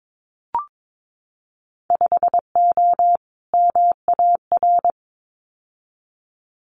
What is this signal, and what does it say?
Morse code, heard as a keyed tone, about 22 wpm, 700 Hz. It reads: 5O MAR
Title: Winter Field Day Exchanges Encore 22wpm